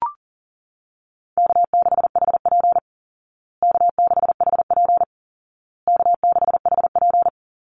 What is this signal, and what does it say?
Morse code, heard as a keyed tone, about 40 wpm, 700 Hz. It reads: K6HP K6HP K6HP
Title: Easy Call Signs Repeated 3x - 40wpm